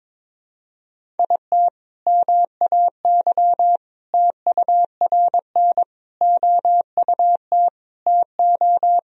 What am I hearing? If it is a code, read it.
Morse code, heard as a keyed tone, about 22 wpm, 700 Hz. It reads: IT MAY TURN OUT TO
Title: Sentences from Top 400 Words and Abbreviations 22wpm